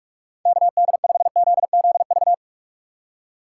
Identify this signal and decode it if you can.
Morse code, heard as a keyed tone, about 56 wpm, 700 Hz. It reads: KD5ZZV
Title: Speed Racing - USA 2x3 Call Signs 45wpm